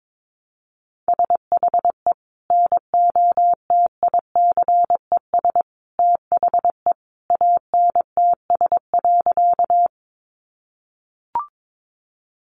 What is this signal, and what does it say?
Morse code, heard as a keyed tone, about 22 wpm, 700 Hz. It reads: SHE NOTICES THE ANTS.
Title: Additional Easy Phrases 22wpm